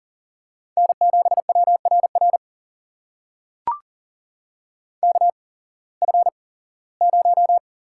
Morse code, heard as a keyed tone, about 42 wpm, spaced as for 21 wpm, 700 Hz. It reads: N7WRR KF0
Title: USA Call Signs - 8x Character Spacing - 40wpm